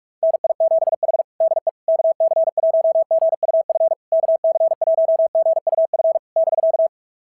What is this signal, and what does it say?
Morse code, heard as a keyed tone, about 44 wpm, 690 Hz. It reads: NI7H DE KC1GUF KC1GUF <BK>
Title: SOTA QSO Practice 45wpm